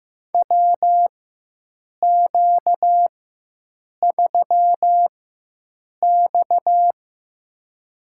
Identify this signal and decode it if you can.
Morse code, heard as a keyed tone, about 15 wpm, spaced as for 7 wpm, 700 Hz. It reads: WQ3X
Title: USA Call Signs - 4 Characters - 4x Character Spacing - 15wpm